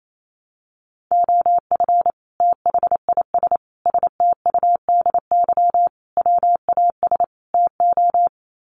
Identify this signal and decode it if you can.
Morse code, heard as a keyed tone, about 28 wpm, 700 Hz. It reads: OF THIS STUDY WAS TO
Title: Sets of 5 Words 28wpm